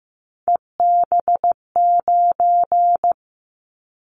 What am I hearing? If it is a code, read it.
Morse code, heard as a keyed tone, about 15 wpm, 700 Hz. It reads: EB9
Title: Callsign Prefixes for Top 100 DXCC - 15wpm